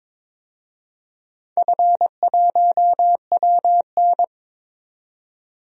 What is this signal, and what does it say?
Morse code, heard as a keyed tone, about 22 wpm, 700 Hz. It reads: F1WN
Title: DXCC-100 Call Signs - 4 Characters - 22wpm